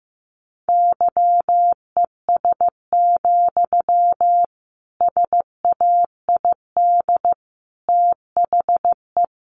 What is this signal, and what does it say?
Morse code, heard as a keyed tone, about 15 wpm, 700 Hz. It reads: YES, SAID THE
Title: Sentences from Top 500 Words 15wpm